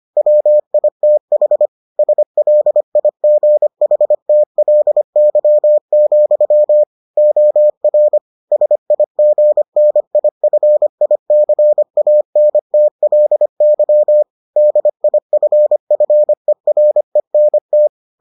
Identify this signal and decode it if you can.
Morse code, heard as a keyed tone, about 25 wpm, 600 Hz. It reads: WITH SLIGHTLY, OR SIGNIFICANTLY DIFFERENT